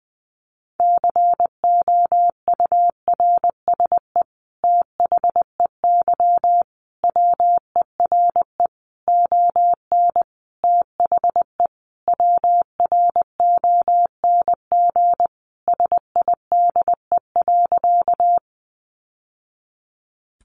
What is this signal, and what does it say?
Morse code, heard as a keyed tone, about 20 wpm, 700 Hz. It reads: COURSE THEY WERE ON THE WRONG SIDE.
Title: Sentences from Top 900 Words 20wpm